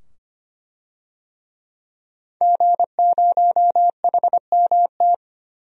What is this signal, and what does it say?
Morse code, heard as a keyed tone, about 25 wpm, 700 Hz. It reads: G0HMT
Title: DXCC-100 Call Signs - 5 Characters - 25wpm